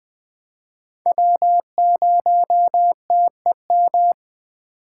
Morse code, heard as a keyed tone, about 20 wpm, 700 Hz. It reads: W0TEM
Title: USA Call Signs - 5 Characters - 6x Character Spacing - 20wpm